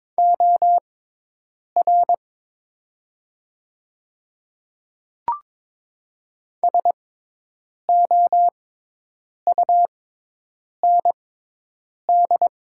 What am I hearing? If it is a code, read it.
Morse code, heard as a keyed tone, about 22 wpm, spaced as for 8 wpm, 700 Hz. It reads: OR SOUND
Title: 6x Character Spacing - Top 100 Words - 22wpm